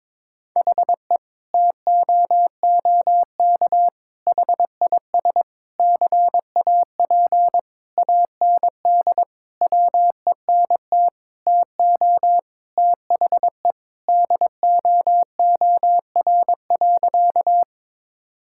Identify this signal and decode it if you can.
Morse code, heard as a keyed tone, about 21 wpm, 700 Hz. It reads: HE TOOK HIS CAP AND WENT TO THE DOOR.
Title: Sentences from Top 1600 Words 22wpm